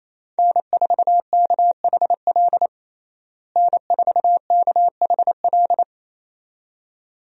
Morse code, heard as a keyed tone, about 28 wpm, 700 Hz. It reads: N4KHL N4KHL
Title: USA Call Signs - 5 Characters - Repeated 2x - 28wpm